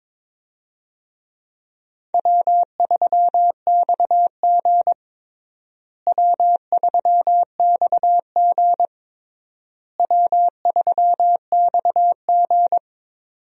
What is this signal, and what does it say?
Morse code, heard as a keyed tone, about 22 wpm, 700 Hz. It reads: W3XG W3XG W3XG
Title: USA Call Signs - 4 Characters - Repeated 3x - 22wpm